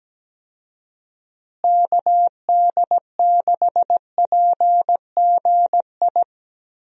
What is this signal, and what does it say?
Morse code, heard as a keyed tone, about 17 wpm, 700 Hz. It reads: KD6PGI
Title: USA Call Signs - 6 Characters - Repeated 3x - 17wpm